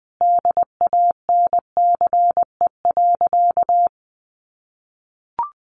Morse code, heard as a keyed tone, about 20 wpm, 700 Hz. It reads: DANCE.
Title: Sentences from Top 700 Words 20wpm